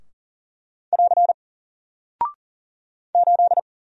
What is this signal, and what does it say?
Morse code, heard as a keyed tone, about 40 wpm, 700 Hz. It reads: <AR> 8
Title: Single Character (Letter:Number:Punctuation:Prosign) Alt 40wpm